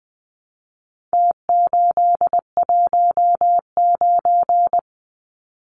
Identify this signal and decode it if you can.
Morse code, heard as a keyed tone, about 20 wpm, 700 Hz. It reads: T819
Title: Numbers - Contest Style - 4-Digit 20wpm